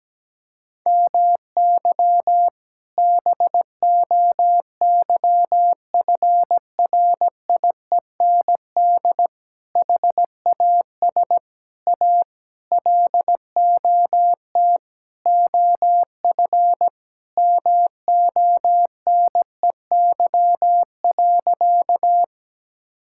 Morse code, heard as a keyed tone, about 17 wpm, 700 Hz. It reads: MY BOYFRIEND HAS A LOT OF MONEY.
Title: Sentences from Top 1000 Words - Review Part 5 - 17wpm